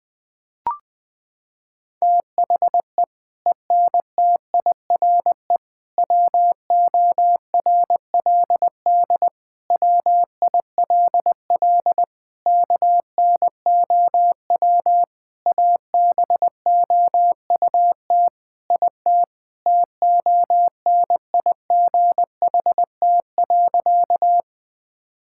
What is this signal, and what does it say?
Morse code, heard as a keyed tone, about 20 wpm, 700 Hz. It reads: THE ENTIRE WORLD WILL KNOW ABOUT IT TONIGHT.
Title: Sentences from Top 1000 Words - Review Part 5 - 20wpm